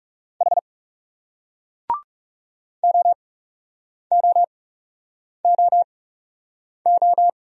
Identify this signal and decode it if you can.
Morse code, heard as a keyed tone, about 40 wpm, spaced as for 8 wpm, 700 Hz. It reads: H OOOO